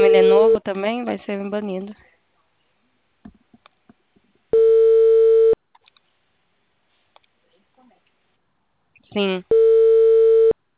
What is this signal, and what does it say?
Ringback tone (China, 450 Hz), with speech.
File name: out- -2003- - - .92